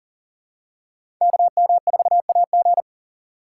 Morse code, heard as a keyed tone, about 40 wpm, 700 Hz. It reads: KM4AG
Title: USA Call Signs - 3x Character Spacing - 40wpm